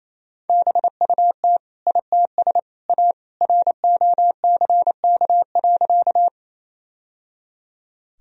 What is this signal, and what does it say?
Morse code, heard as a keyed tone, about 28 wpm, 700 Hz. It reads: BUT ITS A ROCK.
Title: Sentences from Top 1000 Words - Review Part 2 - 28wpm